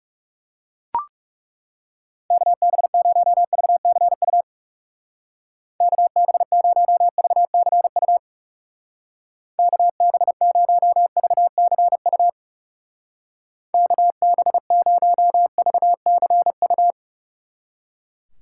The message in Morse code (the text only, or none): KB0VCU KB0VCU KB0VCU KB0VCU